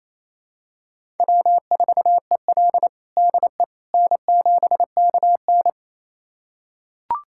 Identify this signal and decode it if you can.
Morse code, heard as a keyed tone, about 28 wpm, 700 Hz. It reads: W4EL DE N7KN
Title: USA Callsign DE Callsign - 4 Characters - 28wpm